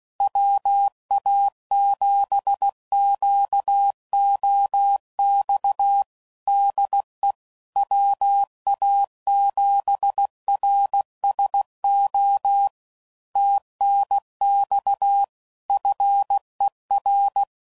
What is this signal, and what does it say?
Morse code, heard as a keyed tone, about 16 wpm, 800 Hz. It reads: WA7QOX DE WA7RSO TNX FER
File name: CW Code Groups @15WPM / QSOs / WA7QOX QRX